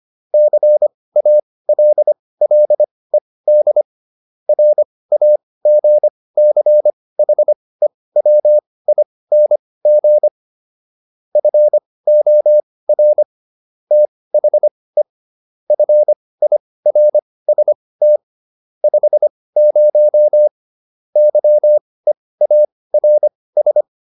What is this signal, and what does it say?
Morse code, heard as a keyed tone, about 25 wpm, spaced as for 18 wpm, 600 Hz. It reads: CALLED RAGCHEWING FOR THE FIRST 50 YEARS